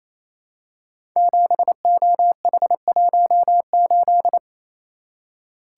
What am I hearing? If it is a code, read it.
Morse code, heard as a keyed tone, about 27 wpm, 700 Hz. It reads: 7OH18
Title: Code Groups - 5 Letter:Numbers - 28wpm